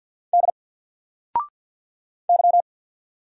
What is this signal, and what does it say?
Morse code, heard as a keyed tone, about 48 wpm, 700 Hz. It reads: D Y
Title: Single Letter-Number 50wpm